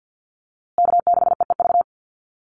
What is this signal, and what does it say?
Morse code, heard as a keyed tone, about 49 wpm, 700 Hz. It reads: K6EEV
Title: USA Call Signs Repeated 2x 50wpm